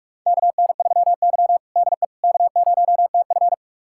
Morse code, heard as a keyed tone, about 46 wpm, 700 Hz. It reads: KN3Y DE K0TF
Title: USA Callsign DE Callsign - 4 Characters - 45wpm